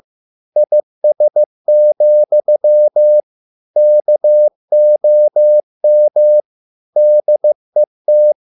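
Morse code, heard as a keyed tone, about 15 wpm, 600 Hz. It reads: IS, KOM DET
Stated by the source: Morse code at a slow 15 wpm from CW med Gnister.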